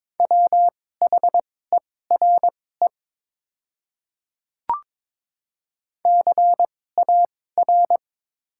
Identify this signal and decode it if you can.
Morse code, heard as a keyed tone, about 22 wpm, spaced as for 16 wpm, 690 Hz. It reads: WHERE CAR
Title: 2x Character Spacing - Top 500 Words - 22wpm